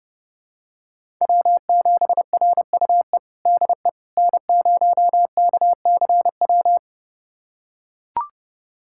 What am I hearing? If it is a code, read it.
Morse code, heard as a keyed tone, about 30 wpm, 700 Hz. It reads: W7RUE DE N0KCW
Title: USA Callsign DE Callsign - 5 Characters - 30wpm